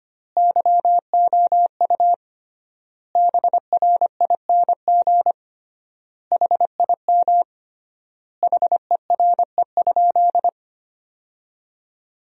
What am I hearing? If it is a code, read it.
Morse code, heard as a keyed tone, about 25 wpm, 700 Hz. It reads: YOU BRING HIM HERE?
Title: 3x Word-Spacing - Sentences from Top 500 Words - 25wpm